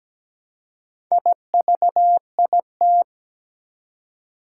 Morse code, heard as a keyed tone, about 17 wpm, 700 Hz. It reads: IVIT
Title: Four Challenging Letter-Number Combinations 17wpm